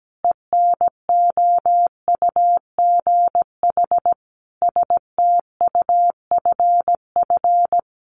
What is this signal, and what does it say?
Morse code, heard as a keyed tone, about 17 wpm, 700 Hz. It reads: ENOUGH STUFF
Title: Sentences from Top 1400 Words 17wpm